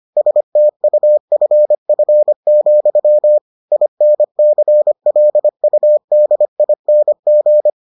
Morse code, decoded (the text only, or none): STUFF, INCLUDING